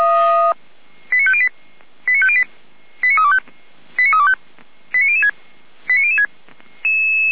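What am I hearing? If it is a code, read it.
Five-tone selective call (ZVEI), heard as quick runs of five tones, tones of about 70 ms. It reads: 89508 89508 89325 89325 89006 89006
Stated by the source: Five-tone ZVEI selective call received on an SDR, from audio.f85360.feuerwehr-orlamuende.org, a fire brigade (Feuerwehr Orlamünde) server.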